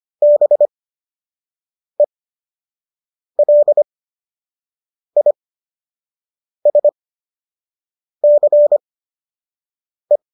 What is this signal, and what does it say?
Morse code, heard as a keyed tone, about 25 wpm, spaced as for 6 wpm, 590 Hz. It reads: BELISCE
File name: qso012 06